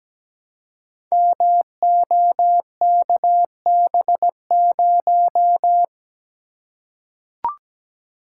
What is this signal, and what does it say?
Morse code, heard as a keyed tone, about 17 wpm, 700 Hz. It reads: MOKB0
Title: Code Groups - 5 Letter:Numbers - 17wpm